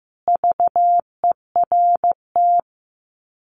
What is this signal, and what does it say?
Morse code, heard as a keyed tone, about 15 wpm, 700 Hz. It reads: VERT
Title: Common QSO Phrases 15wpm